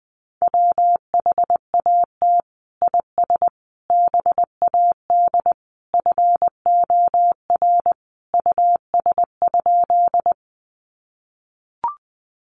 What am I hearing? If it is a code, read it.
Morse code, heard as a keyed tone, about 19 wpm, 700 Hz. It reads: WHAT IS BAD FOR US?
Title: Sentences from Top 700 Words 20wpm